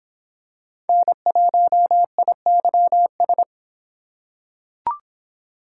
Morse code, heard as a keyed tone, about 24 wpm, 700 Hz. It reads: N1IYS